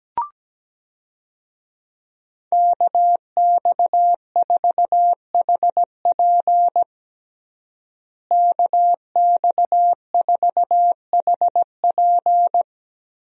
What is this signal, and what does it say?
Morse code, heard as a keyed tone, about 17 wpm, 700 Hz. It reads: KX4HP KX4HP
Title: USA Call Signs - 5 Characters - Repeated 3x - 17wpm